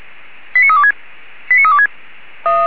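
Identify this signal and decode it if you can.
Five-tone selective call (ZVEI), heard as quick runs of five tones, tones of about 70 ms. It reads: 89326 89326
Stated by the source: ZVEI five-tone call received by SDR.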